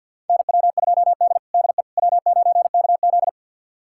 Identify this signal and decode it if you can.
Morse code, heard as a keyed tone, about 52 wpm, 700 Hz. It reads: NW2D DE W9KZ
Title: USA Callsign DE Callsign - 4 Characters - 50wpm